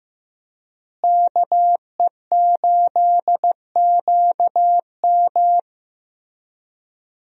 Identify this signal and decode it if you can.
Morse code, heard as a keyed tone, about 16 wpm, 700 Hz. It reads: KE8QM
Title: USA Call Signs - 5 Characters - Repeated 3x - 15wpm